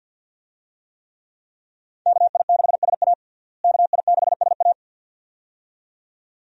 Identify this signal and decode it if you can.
Morse code, heard as a keyed tone, about 47 wpm, 700 Hz. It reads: KI6SA KI6SA
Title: USA Call Signs - 5 Characters - Repeated 2x - 50wpm